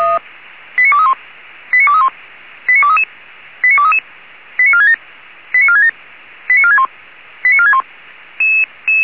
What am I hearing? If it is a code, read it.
Five-tone selective call (ZVEI), heard as quick runs of five tones, tones of about 80 ms. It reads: 89231 89231 89233 89233 89467 89467 89461 89461
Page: SDR/ZVEI